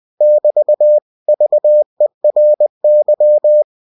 Morse code, heard as a keyed tone, about 20 wpm, 600 Hz. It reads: <BT> VERY
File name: cw-books / winnie-the-pooh / winnie-20wpm-page-0037